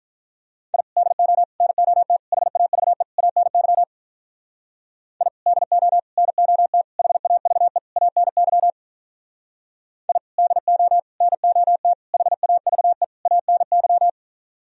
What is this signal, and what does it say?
Morse code, heard as a keyed tone, about 46 wpm, 700 Hz. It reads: I DO NOT HAVE ANY I DO NOT HAVE ANY I DO NOT HAVE ANY